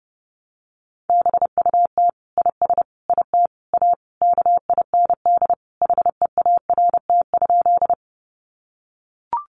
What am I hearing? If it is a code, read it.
Morse code, heard as a keyed tone, about 30 wpm, 700 Hz. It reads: BUT IS IT A KIND HEART?
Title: Sentences from Top 1200 Words 30wpm